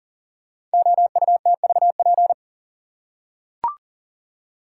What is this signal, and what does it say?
Morse code, heard as a keyed tone, about 39 wpm, 690 Hz. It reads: OUTVP
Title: Code Groups - 5 Letter:Numbers - 40wpm